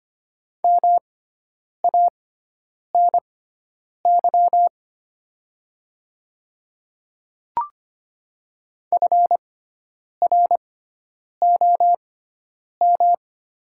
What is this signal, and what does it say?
Morse code, heard as a keyed tone, about 25 wpm, spaced as for 9 wpm, 700 Hz. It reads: MANY FROM
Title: 6x Character Spacing - Top 200 Words - 25wpm